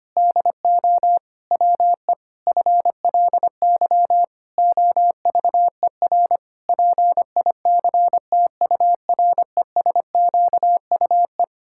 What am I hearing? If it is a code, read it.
Morse code, heard as a keyed tone, about 25 wpm, 700 Hz. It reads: DO WE FLY OVER PICTURESQUE
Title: Pangram Challenge v2 - 25wpm